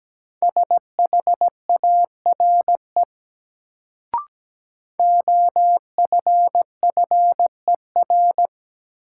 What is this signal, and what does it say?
Morse code, heard as a keyed tone, about 17 wpm, 700 Hz. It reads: SHARE OFFER
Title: Sentences from Top 900 Words - New Word Drill - 17wpm